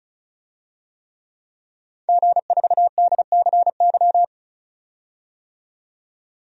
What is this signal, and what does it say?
Morse code, heard as a keyed tone, about 36 wpm, 700 Hz. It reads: G4DCY
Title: DXCC-100 Call Signs - 5 Characters - 35wpm